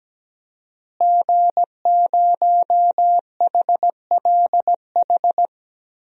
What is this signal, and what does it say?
Morse code, heard as a keyed tone, about 17 wpm, 700 Hz. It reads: G0HLH
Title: DXCC-100 Call Signs - 5 Characters - Repeated 3x - 17wpm